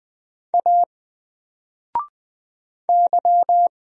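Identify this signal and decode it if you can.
Morse code, heard as a keyed tone, about 20 wpm, 700 Hz. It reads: A Y
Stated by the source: Single letter and number Morse drill at 20 wpm, from N4LCD.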